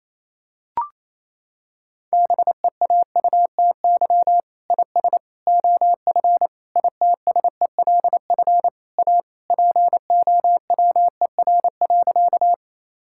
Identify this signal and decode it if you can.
Morse code, heard as a keyed tone, about 28 wpm, 700 Hz. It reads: BEAUTY IS OF ITSELF A POWER.
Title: Sentences from Top 1000 Words - Review Part 5 - 28wpm